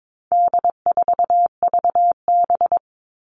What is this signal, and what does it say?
Morse code, heard as a keyed tone, about 22 wpm, 700 Hz. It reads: D4VB
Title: All Cut Numbers - 4-Digit 22wpm